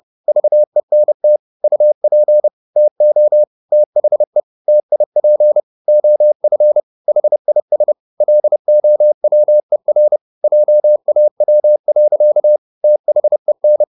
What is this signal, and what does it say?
Morse code, heard as a keyed tone, about 30 wpm, 600 Hz. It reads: VENT UP TO THE TIP OF HIS LOWER JAW. THEN